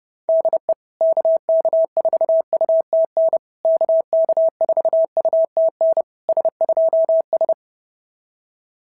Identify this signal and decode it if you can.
Morse code, heard as a keyed tone, about 30 wpm, 690 Hz. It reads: DE KK4UTN KK4UTN S2S
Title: SOTA QSO Practice 30wpm